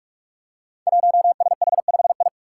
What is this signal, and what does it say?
Morse code, heard as a keyed tone, about 44 wpm, 700 Hz. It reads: 1SH5I
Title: Five Challenging Letter-Number Combinations 45wpm